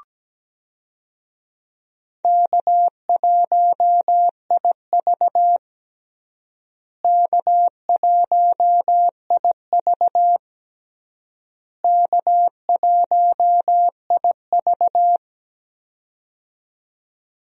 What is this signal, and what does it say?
Morse code, heard as a keyed tone, about 17 wpm, 700 Hz. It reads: K1IV K1IV K1IV
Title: USA Call Signs - 4 Characters - Repeated 3x - 17wpm